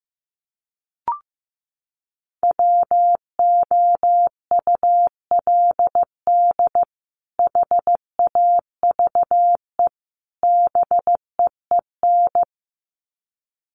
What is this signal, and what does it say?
Morse code, heard as a keyed tone, about 15 wpm, 700 Hz. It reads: WOULD HAVE BEEN
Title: Sets of 3 Words 15wpm